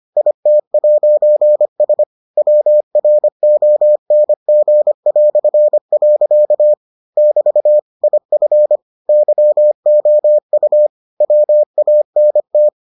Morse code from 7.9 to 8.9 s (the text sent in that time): IF